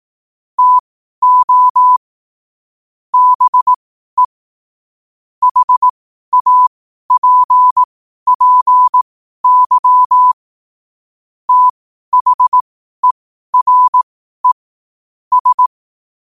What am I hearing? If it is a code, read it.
Morse code, heard as a keyed tone, about 18 wpm, spaced as for 12 wpm, 1000 Hz. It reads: TO BE HAPPY THERE S
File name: qotdfiles / 12wpm